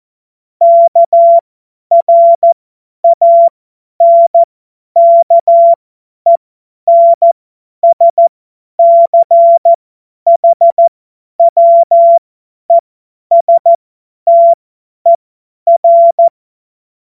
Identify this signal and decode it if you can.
Morse code, heard as a keyed tone, about 14 wpm, spaced as for 10 wpm, 680 Hz. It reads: KRANKENSCHWESTER
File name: Morsekurs / Witze / Witz 11. Juli 2023 50 F70